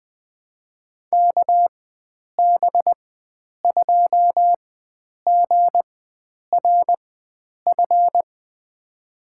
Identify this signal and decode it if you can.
Morse code, heard as a keyed tone, about 21 wpm, spaced as for 9 wpm, 700 Hz. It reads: KB2GRF
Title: USA Call Signs - 6 Characters - 4x Character Spacing - 20wpm